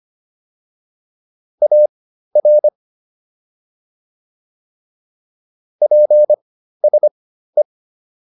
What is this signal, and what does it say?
Morse code, heard as a keyed tone, about 25 wpm, spaced as for 13 wpm, 600 Hz. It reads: AR PSE